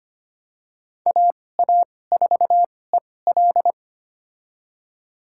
Morse code, heard as a keyed tone, about 25 wpm, spaced as for 18 wpm, 700 Hz. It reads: AA4EL
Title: USA Call Signs - 5 Characters - 2x Character Spacing - 25wpm